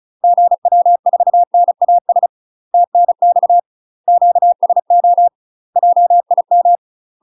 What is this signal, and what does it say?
Morse code, heard as a keyed tone, about 36 wpm, 700 Hz. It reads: GW4NAS TNX QSO JIM
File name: UK MASTER QSOs TRAINING / 35WPM / UK MASTER QSOs 0018